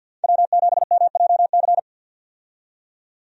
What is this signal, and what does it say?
Morse code, heard as a keyed tone, about 46 wpm, 700 Hz. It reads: W7MJC
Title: USA Call Signs - 5 Characters - 50wpm